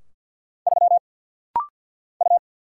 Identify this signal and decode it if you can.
Morse code, heard as a keyed tone, about 50 wpm, 700 Hz. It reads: 3 U
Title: Single Letter-Number Alt - Rapid-Fire 50wpm